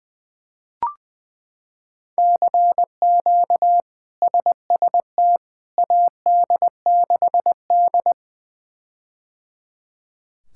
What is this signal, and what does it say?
Morse code, heard as a keyed tone, about 20 wpm, 700 Hz. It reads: CQ SST AD6D
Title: K1USN SST QSO Practice - 20wpm